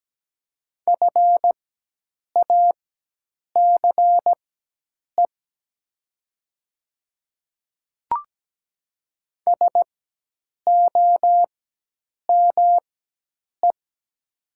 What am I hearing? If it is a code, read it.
Morse code, heard as a keyed tone, about 17 wpm, spaced as for 8 wpm, 700 Hz. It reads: FACE SOME
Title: 4x Character Spacing - Top 500 Words - 17wpm